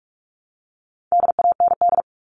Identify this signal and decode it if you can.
Morse code, heard as a keyed tone, about 44 wpm, 700 Hz. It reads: DAND
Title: All Cut Numbers - 4-Digit 45wpm